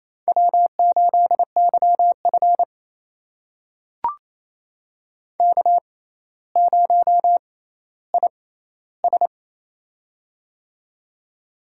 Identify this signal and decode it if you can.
Morse code, heard as a keyed tone, about 28 wpm, spaced as for 14 wpm, 700 Hz. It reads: W8YF K0IS
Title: USA Call Signs - 4 Characters - 6x Character Spacing - 28wpm